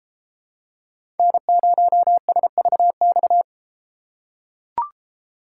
Morse code, heard as a keyed tone, about 32 wpm, 700 Hz. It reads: N0SVX